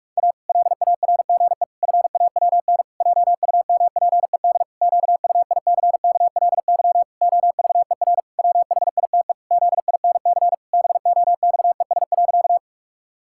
Morse code, heard as a keyed tone, about 46 wpm, 700 Hz. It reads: A LARGE FAWN JUMPED QUICKLY OVER WHITE ZINC BOXES.
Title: Pangram Challenge v2 - 45wpm